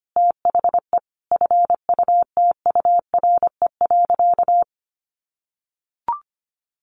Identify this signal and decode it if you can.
Morse code, heard as a keyed tone, about 25 wpm, 700 Hz. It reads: THE FUTURE.
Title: Sentences from Top 700 Words 25wpm